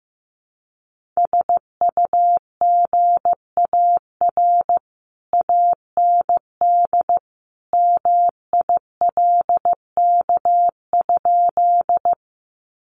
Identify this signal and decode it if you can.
Morse code, heard as a keyed tone, about 15 wpm, 700 Hz. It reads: SUGAR AND MILK?
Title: Sentences from Top 1500 Words 15wpm